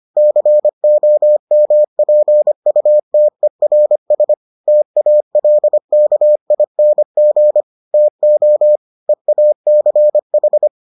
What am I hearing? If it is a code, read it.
Morse code, heard as a keyed tone, about 25 wpm, 600 Hz. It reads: COMPUTERS TALKING TO EACH